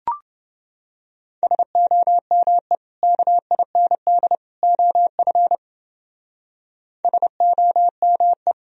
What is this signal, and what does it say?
Morse code, heard as a keyed tone, about 29 wpm, 700 Hz. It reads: SOME KIND OF SOME